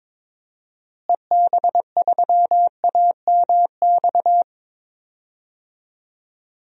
Morse code, heard as a keyed tone, about 22 wpm, 700 Hz. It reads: EB3AMX
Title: DXCC-100 Call Signs - 6 Characters - Repeated 3x - 22wpm